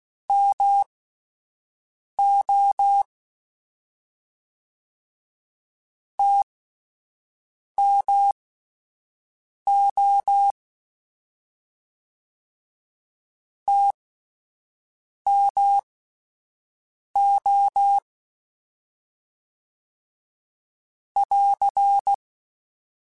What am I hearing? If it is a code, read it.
Morse code, heard as a keyed tone, about 16 wpm, spaced as for 5 wpm, 800 Hz. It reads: MO TMO TMO <AR>
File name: Short Runs / Short Runs-mp3s / All DAHs @10